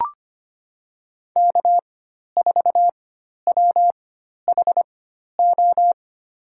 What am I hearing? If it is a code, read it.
Morse code, heard as a keyed tone, about 25 wpm, spaced as for 12 wpm, 700 Hz. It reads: K4WHO
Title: USA Call Signs - 5 Characters - 4x Character Spacing - 25wpm